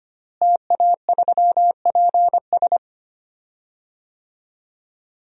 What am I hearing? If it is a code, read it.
Morse code, heard as a keyed tone, about 25 wpm, 700 Hz. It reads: TA3PS
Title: DXCC-100 Call Signs - 5 Characters - Repeated 3x - 25wpm